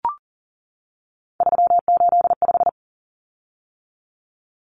Morse code, heard as a keyed tone, about 40 wpm, 700 Hz. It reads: 385
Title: Common Cut Numbers - 3-Digit 40wpm